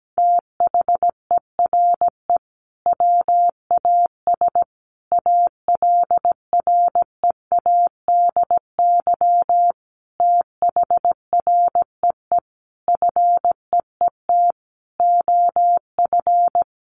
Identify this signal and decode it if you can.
Morse code, heard as a keyed tone, about 17 wpm, 700 Hz. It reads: THERE WAS ALREADY THREE FEET OF
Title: Sentences from Top 700 Words 17wpm